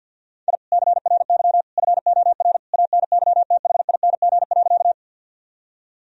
Morse code, heard as a keyed tone, about 51 wpm, 700 Hz. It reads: I CRY FOR ANYTHING.
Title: Sentences from Top 1000 Words - Review Part 2 - 50wpm